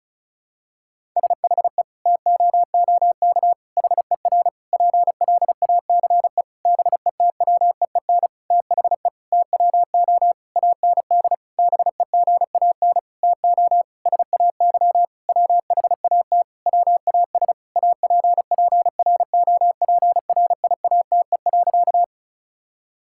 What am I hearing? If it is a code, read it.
Morse code, heard as a keyed tone, about 36 wpm, 700 Hz. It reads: SHE TOOK HER PLACE BETWEEN THE TWO AND BEGAN TO SAY WHAT WAS APPROPRIATE.
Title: Sentences from Top 1000 Words - Review Part 5 - 35wpm